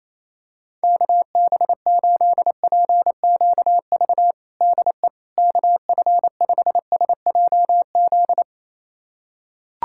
Morse code, heard as a keyed tone, about 28 wpm, 700 Hz. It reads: KB8PQV DE KF5SJZ
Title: USA Callsign DE Callsign - 6 Characters - 28wpm